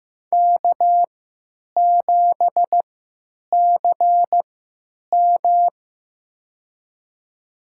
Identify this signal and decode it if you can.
Morse code, heard as a keyed tone, about 15 wpm, spaced as for 9 wpm, 700 Hz. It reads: K7CM
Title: USA Call Signs - 4 Characters - 3x Character Spacing - 15wpm